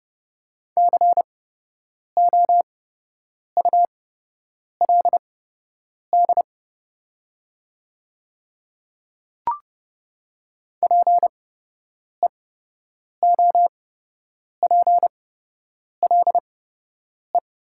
Morse code, heard as a keyed tone, about 30 wpm, spaced as for 8 wpm, 700 Hz. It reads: COULD PEOPLE
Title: 8x Character Spacing - Top 100 Words - 30wpm